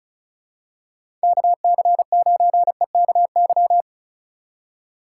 Morse code, heard as a keyed tone, about 36 wpm, 700 Hz. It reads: KC9EKY
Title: USA Call Signs - 6 Characters - 3x Character Spacing - 35wpm